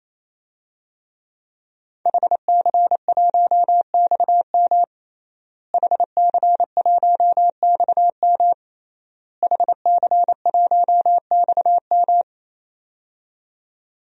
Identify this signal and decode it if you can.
Morse code, heard as a keyed tone, about 28 wpm, 700 Hz. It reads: HC1XM HC1XM HC1XM
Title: DXCC-100 Call Signs - 5 Characters - Repeated 3x - 28wpm